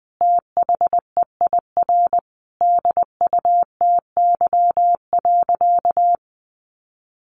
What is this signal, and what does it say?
Morse code, heard as a keyed tone, about 20 wpm, 700 Hz. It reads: THEIR DUTY.
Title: Sentences from Top 1500 Words 20wpm